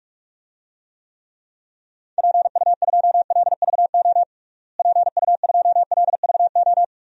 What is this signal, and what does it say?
Morse code, heard as a keyed tone, about 46 wpm, 700 Hz. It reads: PU2LVO PU2LVO
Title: DXCC-100 Call Signs - 6 Characters - Repeated 3x - 45wpm